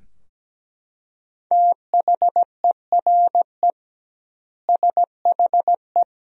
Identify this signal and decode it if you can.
Morse code, heard as a keyed tone, about 17 wpm, 700 Hz. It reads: THERE SHE
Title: 2x Word-Spacing - Sentences from Top 200 Words - 17wpm